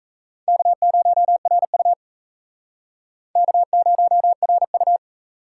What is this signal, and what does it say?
Morse code, heard as a keyed tone, about 36 wpm, 700 Hz. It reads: K0RU K0RU